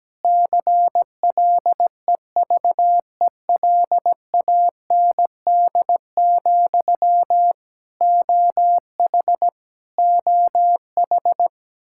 Morse code, heard as a keyed tone, about 17 wpm, 700 Hz. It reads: CLEVELAND, OH OH
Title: Practice QSO - Exchange 1 - Beginning - 17wpm